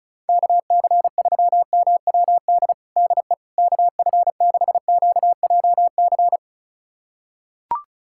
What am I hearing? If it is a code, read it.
Morse code, heard as a keyed tone, about 36 wpm, 700 Hz. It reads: KC3MWD DE KF6QJC
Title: USA Callsign DE Callsign 35wpm